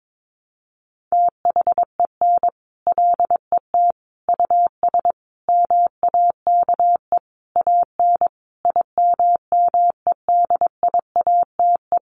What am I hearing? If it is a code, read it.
Morse code, heard as a keyed tone, about 22 wpm, 700 Hz. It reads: THEN LET US MAKE AN IMMEDIATE
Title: Sentences from Top 1200 Words 22wpm